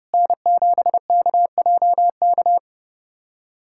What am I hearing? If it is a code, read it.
Morse code, heard as a keyed tone, about 30 wpm, 690 Hz. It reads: N7KJK
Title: SOTA QSO Practice 30wpm